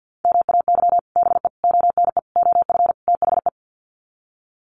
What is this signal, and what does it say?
Morse code, heard as a keyed tone, about 50 wpm, 700 Hz. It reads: MAY BE ONE OF THE
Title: Sets of 5 Words Encore2 50wpm